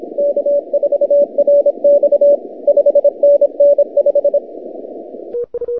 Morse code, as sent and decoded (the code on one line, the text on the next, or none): -.- ....- .-. -..- / ..... -. -. .....
K4RX 5NN5